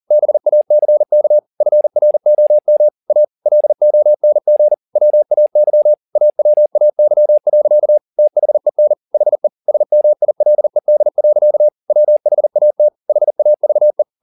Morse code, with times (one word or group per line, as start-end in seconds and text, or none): 0.1-1.4 BACK
1.6-2.9 FROM
3.1-3.2 A
3.5-4.7 LONG
5.0-5.9 WAY
6.2-8.0 AWAY.
8.2-8.9 THEN
9.1-9.5 HE
9.7-11.7 SMILED.
11.9-12.9 WHAT
13.1-14.0 HAVE